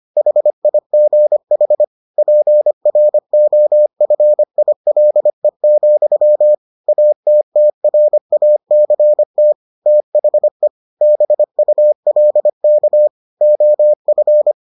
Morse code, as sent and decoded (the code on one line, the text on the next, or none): .... .. --. .... / .--. .-. --- ..-. .. .-.. --..-- / .- - - .-. .- -.-. - / - .... / -... ..- .-.. -.- / --- ..-.
HIGH PROFILE, ATTRACT THE BULK OF